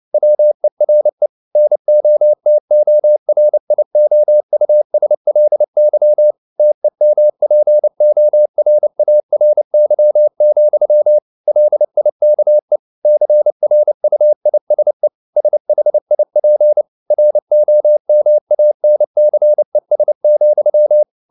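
Morse code, heard as a keyed tone, about 29 wpm, 600 Hz. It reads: WERE NOTORIOUSLY TEMPORARY, LIKE CRUISE SHIP ROMANCES,
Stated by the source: Morse code at 29 wpm, sending Ken Follett's novel Never in English, from CW med Gnister.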